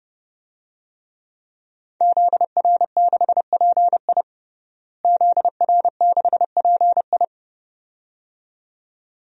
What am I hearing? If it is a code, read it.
Morse code, heard as a keyed tone, about 30 wpm, 700 Hz. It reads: ZR6PI ZR6PI
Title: DXCC-100 Call Signs - 5 Characters - Repeated 2x - 30wpm